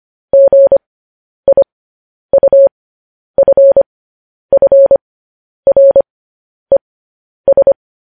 Morse code, heard as a keyed tone, about 25 wpm, spaced as for 10 wpm, 550 Hz. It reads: GIUFFRES
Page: news headlines in morse code at 10 wpm